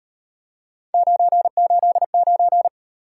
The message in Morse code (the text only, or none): 989